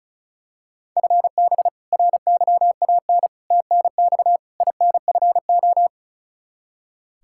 Morse code, heard as a keyed tone, about 34 wpm, 700 Hz. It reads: FB RYAN TNX INFO
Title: K1USN SST QSO Practice 35wpm